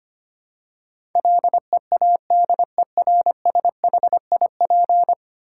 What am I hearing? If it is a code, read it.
Morse code, heard as a keyed tone, about 25 wpm, 700 Hz. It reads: LEADERSHIP
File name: 10 Letter Words - First 3 Letters Given - 25wpm